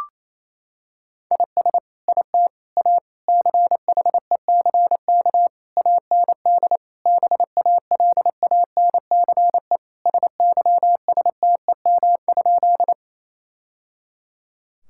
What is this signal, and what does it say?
Morse code, heard as a keyed tone, about 28 wpm, 700 Hz. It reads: IS IT A CHECK AND BALANCE SYSTEM?
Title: Sentences from Top 1000 Words - Review Part 6 - 28wpm